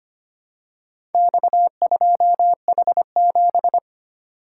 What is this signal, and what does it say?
Morse code, heard as a keyed tone, about 25 wpm, 700 Hz. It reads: X2H7
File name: practice / 25wpm / Four Letter-Number Combinations - 25wpm